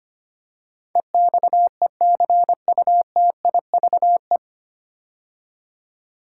Morse code, heard as a keyed tone, about 25 wpm, 700 Hz. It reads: EXECUTIVE
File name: practice / 25wpm / 9 Letter Words - First 3 Letters Given - 25wpm